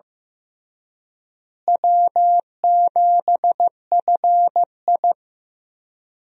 Morse code, heard as a keyed tone, about 15 wpm, 700 Hz. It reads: W7FI
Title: USA Call Signs - 4 Characters - Repeated 3x - 15wpm